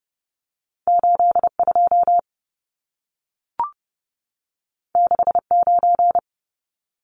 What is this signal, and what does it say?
Morse code, heard as a keyed tone, about 30 wpm, 700 Hz. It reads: 82 69
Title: Numbers - 2-Digit 30wpm